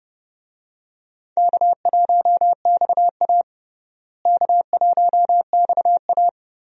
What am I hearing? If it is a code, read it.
Morse code, heard as a keyed tone, about 32 wpm, 700 Hz. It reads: K1XA K1XA
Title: USA Call Signs - 4 Characters - Repeated 2x - 30wpm